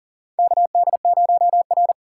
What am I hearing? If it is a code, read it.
Morse code, heard as a keyed tone, about 40 wpm, 700 Hz. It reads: KD0R
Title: USA Call Signs - 6x Character Spacing - 40wpm